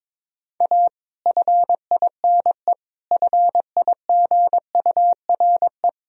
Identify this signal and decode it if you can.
Morse code, heard as a keyed tone, about 21 wpm, 700 Hz. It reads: A FINE FIGURE
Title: Sentences from Top 1000 Words - Review Part 2 - 22wpm